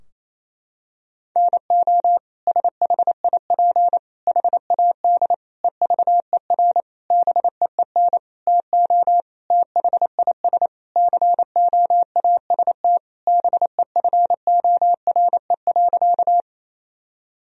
Morse code, heard as a keyed tone, about 28 wpm, 700 Hz. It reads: NO SHIP HAD EVER BEEN TO THIS COAST BEFORE.
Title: Sentences from Top 1000 Words - Review Part 4 - 28wpm